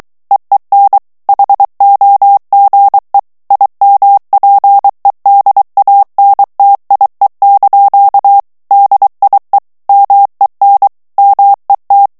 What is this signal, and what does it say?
Morse code, heard as a keyed tone, about 23 wpm, 800 Hz. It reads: EEN HOGE IMPEDANTIE) DIE MEN MET
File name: HOLLANDAIS texte clair 1 vitesse 22 mots minute